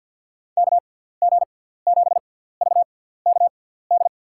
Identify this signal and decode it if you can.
Morse code, heard as a keyed tone, about 47 wpm, spaced as for 17 wpm, 700 Hz. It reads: KG7VKD
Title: USA Call Signs - 6 Characters - 6x Character Spacing - 50wpm